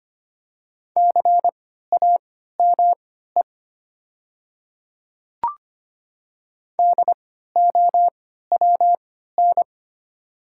Morse code, heard as a keyed tone, about 25 wpm, spaced as for 14 wpm, 690 Hz. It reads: CAME DOWN
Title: 3x Character Spacing - Top 200 Words - 25wpm